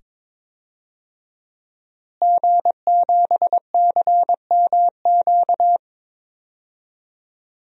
Morse code, heard as a keyed tone, about 22 wpm, 700 Hz. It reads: G7CMQ
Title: DXCC-100 Call Signs - 5 Characters - Repeated 2x - 22wpm